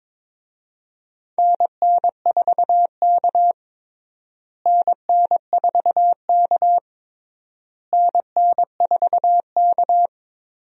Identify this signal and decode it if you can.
Morse code, heard as a keyed tone, about 22 wpm, 700 Hz. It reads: NN4K NN4K NN4K
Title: USA Call Signs - 4 Characters - Repeated 3x - 22wpm